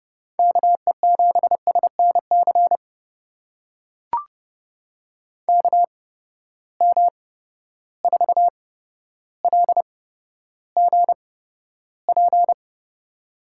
Morse code, heard as a keyed tone, about 32 wpm, spaced as for 13 wpm, 700 Hz. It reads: KE7SNC KM4LGP
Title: USA Call Signs - 6 Characters - 8x Character Spacing - 30wpm